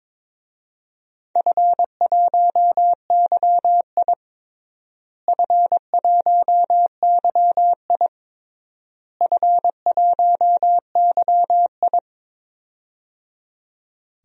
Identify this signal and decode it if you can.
Morse code, heard as a keyed tone, about 22 wpm, 700 Hz. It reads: F1YI F1YI F1YI
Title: DXCC-100 Call Signs - 4 Characters - Repeated 3x - 22wpm